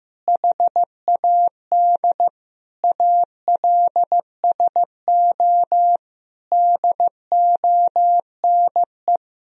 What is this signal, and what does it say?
Morse code, heard as a keyed tone, about 15 wpm, 700 Hz. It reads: HAD ALSO DONE
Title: Sentences from Top 1000 Words - Review Part 1 - 15wpm